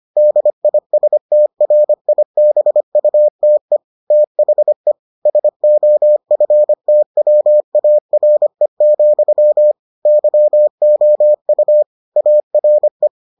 Morse code, heard as a keyed tone, about 25 wpm, 600 Hz. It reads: DISTRIBUTE THE SOFTWARE, YOU ARE